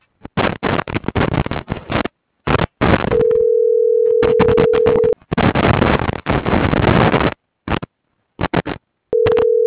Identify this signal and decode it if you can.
Ringback tone (North America), 440 + 480 Hz, with noise.
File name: CENTRAL GENERAL / out tania.lemus